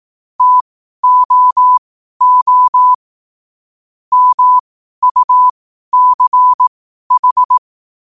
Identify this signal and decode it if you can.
Morse code, heard as a keyed tone, about 18 wpm, spaced as for 12 wpm, 1000 Hz. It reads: TOO MUCH
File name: qotdfiles / 12wpm